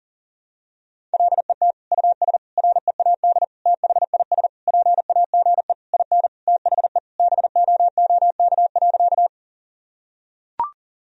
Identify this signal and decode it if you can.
Morse code, heard as a keyed tone, about 40 wpm, 700 Hz. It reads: LET US READ THIS PAGE IN THE BOOK.
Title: Sentences from Top 1000 Words - Review Part 1 - 40wpm